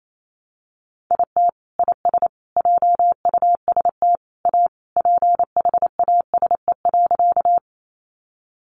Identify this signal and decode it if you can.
Morse code, heard as a keyed tone, about 28 wpm, 700 Hz. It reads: IT IS JUST A PHASE.
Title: Sentences from Top 1200 Words 28wpm